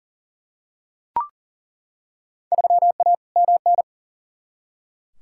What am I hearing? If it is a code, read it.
Morse code, heard as a keyed tone, about 40 wpm, 700 Hz. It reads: 3A MN
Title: Field Day Exchanges - 40wpm